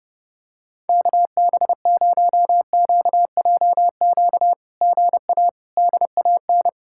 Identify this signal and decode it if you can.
Morse code, heard as a keyed tone, about 30 wpm, 700 Hz. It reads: KB0QJQ GA DAN
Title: SOTA QSO Practice 30wpm